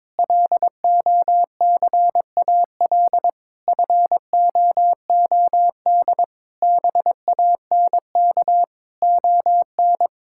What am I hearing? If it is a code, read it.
Morse code, heard as a keyed tone, about 22 wpm, 700 Hz. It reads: LOCAL FOOD BANK ON
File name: Difficult Sentences with Hints - 22wpm